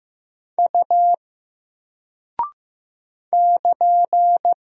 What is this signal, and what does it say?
Morse code, heard as a keyed tone, about 15 wpm, 700 Hz. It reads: U <KN>
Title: Single Character (Letter:Number:Punctuation:Prosign) 15wpm